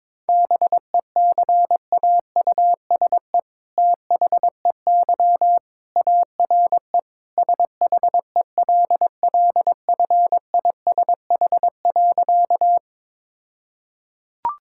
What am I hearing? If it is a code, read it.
Morse code, heard as a keyed tone, about 21 wpm, 700 Hz. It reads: BECAUSE THEY ARE SHELLFISH.
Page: morse-code-ninja-practice-22wpm